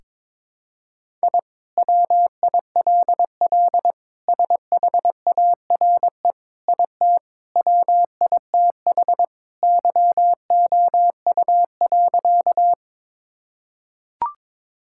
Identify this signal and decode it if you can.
Morse code, heard as a keyed tone, about 21 wpm, 700 Hz. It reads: I WILL SHARE IT WITH YOU.
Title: Sentences from Top 1000 Words - Review Part 6 - 22wpm